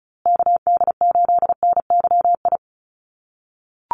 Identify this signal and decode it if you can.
Morse code, heard as a keyed tone, about 34 wpm, 700 Hz. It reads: KD8NYI
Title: USA Call Signs Repeated 3x 35wpm